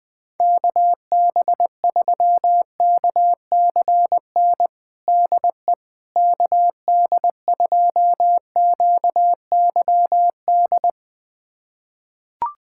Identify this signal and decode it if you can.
Morse code, heard as a keyed tone, about 20 wpm, 700 Hz. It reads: KB3KCN DE KD2QYD
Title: USA Callsign DE Callsign - 6 Characters - 20wpm